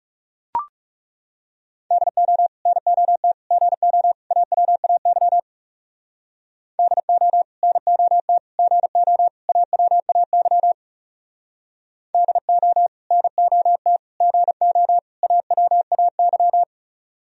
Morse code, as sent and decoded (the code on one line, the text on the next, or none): -.. --- / -. --- - / --. --- / .- .-- .- -.-- / -.. --- / -. --- - / --. --- / .- .-- .- -.-- / -.. --- / -. --- - / --. --- / .- .-- .- -.--
DO NOT GO AWAY DO NOT GO AWAY DO NOT GO AWAY